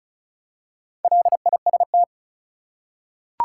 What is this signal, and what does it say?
Morse code, heard as a keyed tone, about 36 wpm, 700 Hz. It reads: LIST
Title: Four Letter Words - 35wpm